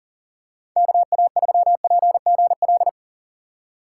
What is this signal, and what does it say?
Morse code, heard as a keyed tone, about 40 wpm, 700 Hz. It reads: KA3PGL
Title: USA Call Signs - 6 Characters - 2x Character Spacing - 40wpm